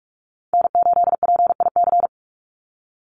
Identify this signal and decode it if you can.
Morse code, heard as a keyed tone, about 44 wpm, 700 Hz. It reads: N8PIC
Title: USA Call Signs 45wpm